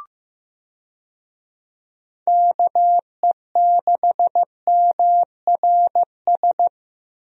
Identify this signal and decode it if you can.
Morse code, heard as a keyed tone, about 15 wpm, 700 Hz. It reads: KE6MRS
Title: USA Call Signs - 6 Characters - 15wpm